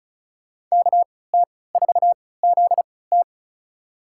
Morse code, heard as a keyed tone, about 36 wpm, spaced as for 20 wpm, 700 Hz. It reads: KT4ZT
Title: USA Call Signs - 5 Characters - 3x Character Spacing - 35wpm